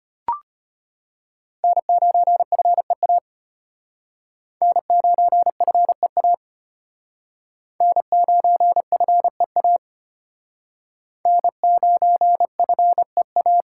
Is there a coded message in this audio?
Yes: Morse code from 1.6 to 13.6 s, heard as a keyed tone, about 33 wpm, 700 Hz. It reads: N9FEA N9FEA N9FEA N9FEA